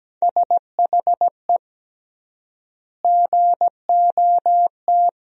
Morse code, heard as a keyed tone, about 17 wpm, 700 Hz. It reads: SHE GOT
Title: 3x Word-Spacing - Sentences from Top 500 Words - 17wpm